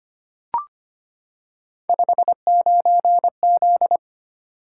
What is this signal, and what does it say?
Morse code, heard as a keyed tone, about 25 wpm, 700 Hz. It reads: 59Z
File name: Three Letter-Number Combinations - 25wpm